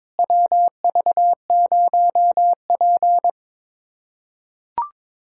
Morse code, heard as a keyed tone, about 22 wpm, 700 Hz. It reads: WV0P
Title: USA Callsign DE Callsign - 4 Characters - 22wpm